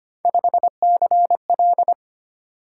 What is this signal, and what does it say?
Morse code, heard as a keyed tone, about 25 wpm, 700 Hz. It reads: 5CL
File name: Three Letter-Number Combinations - 25wpm